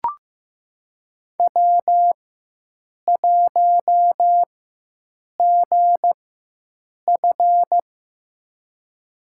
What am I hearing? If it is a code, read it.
Morse code, heard as a keyed tone, about 15 wpm, spaced as for 7 wpm, 700 Hz. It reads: W1GF
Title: USA Call Signs - 4x Character Spacing - 15wpm